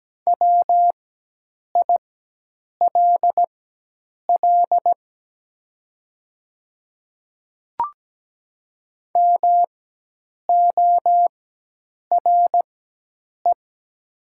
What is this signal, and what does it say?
Morse code, heard as a keyed tone, about 17 wpm, spaced as for 8 wpm, 700 Hz. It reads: WILL MORE
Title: 4x Character Spacing - Top 100 Words - 17wpm